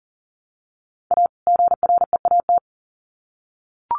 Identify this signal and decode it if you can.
Morse code, heard as a keyed tone, about 40 wpm, 700 Hz. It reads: A GREAT
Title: Sets of 2 Words 40wpm